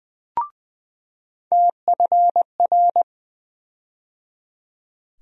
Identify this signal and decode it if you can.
Morse code, heard as a keyed tone, about 20 wpm, 700 Hz. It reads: TFR
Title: USA Call Sign Postfixes - 3 Letters 20wpm